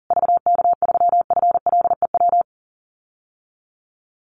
Morse code, heard as a keyed tone, about 40 wpm, 700 Hz. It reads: VK3FLEW
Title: USA Call Signs Repeated 3x 40wpm